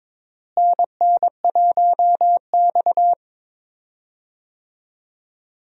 Morse code, heard as a keyed tone, about 22 wpm, 700 Hz. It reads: NN1X
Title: USA Call Signs - 4 Characters - Repeated 2x - 22wpm